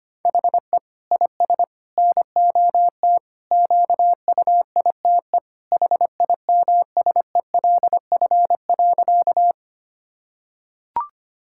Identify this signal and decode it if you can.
Morse code, heard as a keyed tone, about 25 wpm, 700 Hz. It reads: HE IS NOT QUITE HIMSELF.
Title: Sentences from Top 1600 Words 25wpm